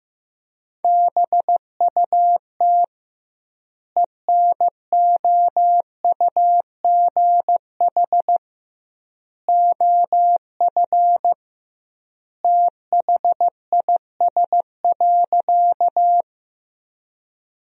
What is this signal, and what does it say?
Morse code, heard as a keyed tone, about 15 wpm, 700 Hz. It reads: BUT ENOUGH OF THIS.
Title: 2x Word-Spacing - Sentences from Top 400 Words - 15wpm